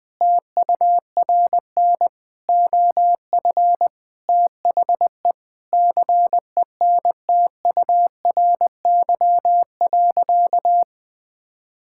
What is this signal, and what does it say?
Morse code, heard as a keyed tone, about 20 wpm, 700 Hz. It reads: TURN OF THE CENTURY.
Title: Sentences with 5-Word Phrases - Spoken Beginning - 20wpm